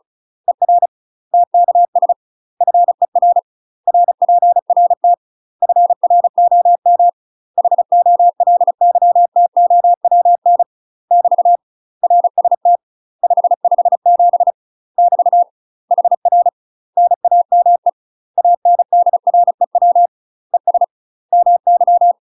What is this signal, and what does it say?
Morse code, heard as a keyed tone, about 34 wpm, 700 Hz. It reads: ER TKS FER RPRT FROM HOLYTOWN <BT> RST 557 <BT> HR NAME ANDREW ES MY
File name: UK MASTER QSOs TRAINING / 35WPM / UK MASTER QSOs 0034